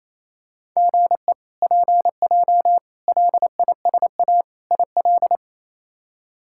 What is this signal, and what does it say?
Morse code, heard as a keyed tone, about 28 wpm, 700 Hz. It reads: GE PJ LISA IL
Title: K1USN SST QSO Practice - 28wpm